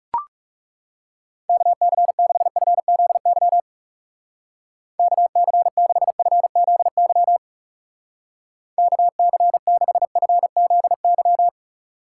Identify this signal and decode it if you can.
Morse code, heard as a keyed tone, about 36 wpm, 700 Hz. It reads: KC6FZY KC6FZY KC6FZY
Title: Speed Racing - USA Call Signs 30wpm